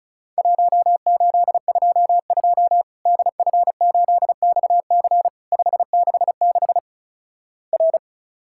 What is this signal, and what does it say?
Morse code, heard as a keyed tone, about 36 wpm, 700 Hz. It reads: 1822 DF8XC 566 R
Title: Worked All Europe DX Contest - QTC Exchange - 35wpm